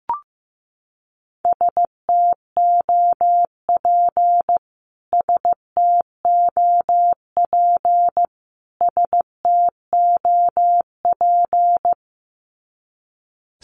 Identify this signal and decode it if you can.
Morse code, heard as a keyed tone, about 15 wpm, 700 Hz. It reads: STOP STOP STOP
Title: Top 500 Words Repeated 15wpm